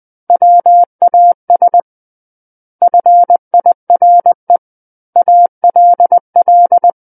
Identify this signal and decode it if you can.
Morse code, heard as a keyed tone, about 20 wpm, 700 Hz. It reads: WAS FIRE ALL
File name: 20 WPM